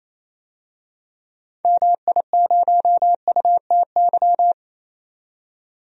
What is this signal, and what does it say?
Morse code, heard as a keyed tone, about 28 wpm, 700 Hz. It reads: MI0UTY
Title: DXCC-100 Call Signs - 6 Characters - 28wpm